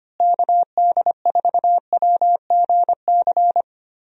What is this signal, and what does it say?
Morse code, heard as a keyed tone, about 25 wpm, 700 Hz. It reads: KD4WGC
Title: USA Call Signs - 6 Characters - Repeated 3x - 25wpm